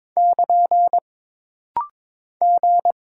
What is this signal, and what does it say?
Morse code, heard as a keyed tone, about 22 wpm, 700 Hz. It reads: <KN> G
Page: This rapid-fire timing is geared to push your ICR (Instant Character Recognition) skills.